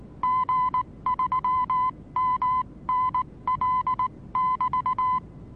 Morse code, heard as a keyed tone, about 19 wpm, spaced as for 16 wpm, 1010 Hz. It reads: G3MNL<BT>
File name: repos / Morsekurs / DJ4UF Morsen26